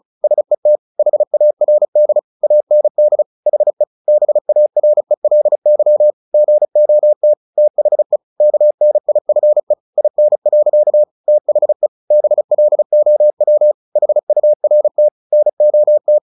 Morse code, heard as a keyed tone, about 34 wpm, 600 Hz. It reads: SET HARD AND HE BARELY GOT THE KNIFE IN. THE BLOW HURT NOT
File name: the old man and the sea 2 35wpm